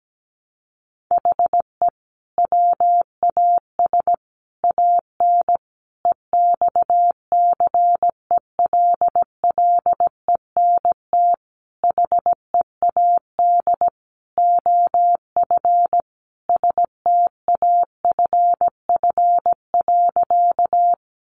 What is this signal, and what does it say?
Morse code, heard as a keyed tone, about 17 wpm, 700 Hz. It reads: HE WAS AN EXCELLENT HEAD OF STAFF.
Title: Sentences from Top 1200 Words 17wpm